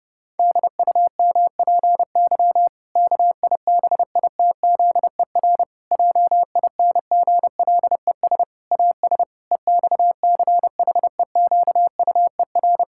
Morse code, heard as a keyed tone, about 30 wpm, 700 Hz. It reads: DUMPY KIBITZER JINGLES AS EXCHEQUER
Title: Pangram Challenge v2 - 30wpm